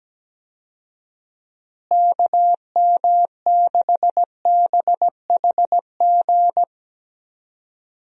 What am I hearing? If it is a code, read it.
Morse code, heard as a keyed tone, about 17 wpm, 700 Hz. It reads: KM6BHG
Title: USA Call Signs - 6 Characters - Repeated 3x - 17wpm